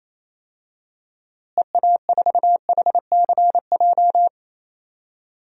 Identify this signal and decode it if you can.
Morse code, heard as a keyed tone, about 27 wpm, 700 Hz. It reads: EA4HCJ
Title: DXCC-100 Call Signs - 6 Characters - Repeated 2x - 28wpm